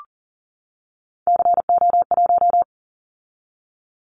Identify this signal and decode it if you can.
Morse code, heard as a keyed tone, about 38 wpm, 700 Hz. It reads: CO1
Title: Callsign Prefixes for Top 100 DXCC - 40wpm